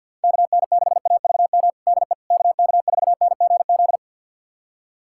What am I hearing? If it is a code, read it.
Morse code, heard as a keyed tone, about 51 wpm, 700 Hz. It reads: KN6AVM DE KK4NGZ
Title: USA Callsign DE Callsign 50wpm